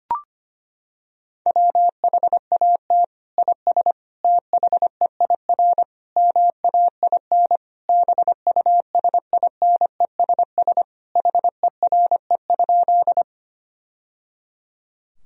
Morse code, heard as a keyed tone, about 25 wpm, 700 Hz. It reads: WHAT IS THEIR MAIN BUSINESS HERE?
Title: Sentences from Top 1000 Words - Review Part 2 - 25wpm